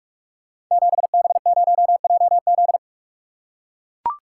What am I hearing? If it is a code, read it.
Morse code, heard as a keyed tone, about 44 wpm, 700 Hz. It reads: 7B0JZ
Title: Code Groups - 5 Letter:Numbers - 45wpm